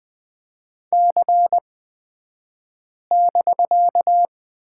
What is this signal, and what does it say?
Morse code, heard as a keyed tone, about 20 wpm, spaced as for 5 wpm, 700 Hz. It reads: C<BK>
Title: 254 Lesson 41d - Period - Two Characters 20wpm